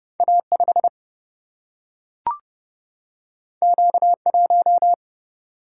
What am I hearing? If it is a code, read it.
Morse code, heard as a keyed tone, about 30 wpm, 700 Hz. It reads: A 5 Q 1
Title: Single-Letter Single-Number 30wpm